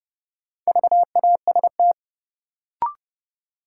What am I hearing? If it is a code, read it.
Morse code, heard as a keyed tone, about 32 wpm, 700 Hz. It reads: VAST
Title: Sentences from Top 1300 Words - New Word Drill - 30wpm